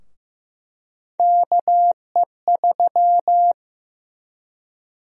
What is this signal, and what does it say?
Morse code, heard as a keyed tone, about 15 wpm, 700 Hz. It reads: KE3
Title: USA Call Sign Prefixes - Repeated 2x - 15wpm